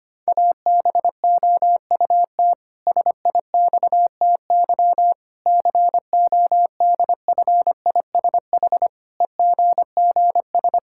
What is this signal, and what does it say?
Morse code, heard as a keyed tone, about 25 wpm, 700 Hz. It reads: ABOUT SIXTY CODFISH EGGS
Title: Pangram Challenge v2 - 25wpm